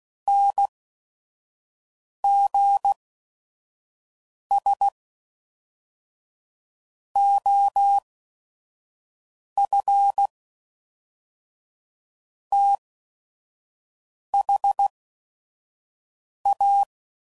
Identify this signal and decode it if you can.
Morse code, heard as a keyed tone, about 16 wpm, spaced as for 5 wpm, 800 Hz. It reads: NGS OF THA
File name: CW Code Groups @5WPM